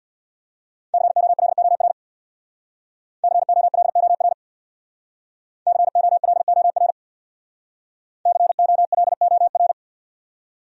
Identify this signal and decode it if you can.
Morse code, heard as a keyed tone, about 67 wpm, 700 Hz. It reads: COLOR COLOR COLOR COLOR